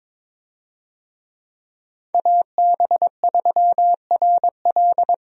Morse code, heard as a keyed tone, about 22 wpm, 700 Hz. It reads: AB3RL
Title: USA Call Signs - 5 Characters - Repeated 3x - 22wpm